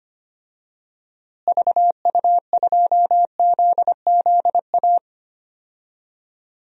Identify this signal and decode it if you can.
Morse code, heard as a keyed tone, about 25 wpm, 700 Hz. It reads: VU2ZZA
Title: DXCC-100 Call Signs - 6 Characters - Repeated 3x - 25wpm